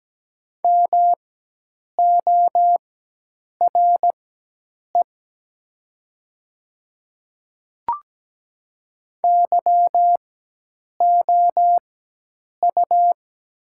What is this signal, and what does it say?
Morse code, heard as a keyed tone, about 17 wpm, spaced as for 8 wpm, 700 Hz. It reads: MORE YOU
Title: 4x Character Spacing - Top 100 Words - 17wpm